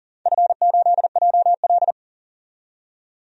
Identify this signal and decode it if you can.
Morse code, heard as a keyed tone, about 40 wpm, 700 Hz. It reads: F8JL
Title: DXCC-100 Call Signs - 4 Characters - 40wpm